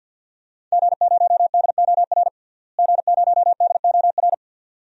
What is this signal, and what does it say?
Morse code, heard as a keyed tone, about 53 wpm, 700 Hz. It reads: G0DOR G0DOR
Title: DXCC-100 Call Signs - 5 Characters - Repeated 2x - 50wpm